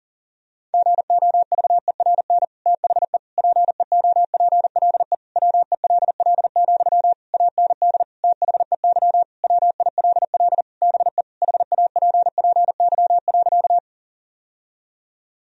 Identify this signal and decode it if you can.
Morse code, heard as a keyed tone, about 40 wpm, 700 Hz. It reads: GOVERN THE PEOPLE WELL, AND THEY WILL BE HAPPY.
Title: Sentences from Top 1000 Words - Review Part 2 - 40wpm